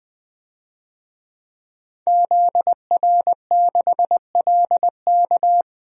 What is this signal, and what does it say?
Morse code, heard as a keyed tone, about 20 wpm, 700 Hz. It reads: ZR6LK
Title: DXCC-100 Call Signs - 5 Characters - 20wpm